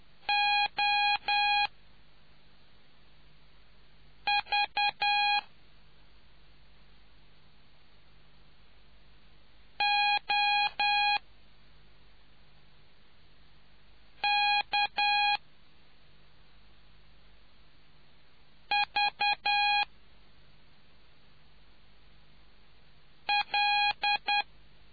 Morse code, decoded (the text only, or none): O V O K V L